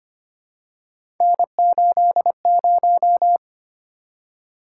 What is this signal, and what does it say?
Morse code, heard as a keyed tone, about 25 wpm, 700 Hz. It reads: N80
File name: Three Letter-Number Combinations - 25wpm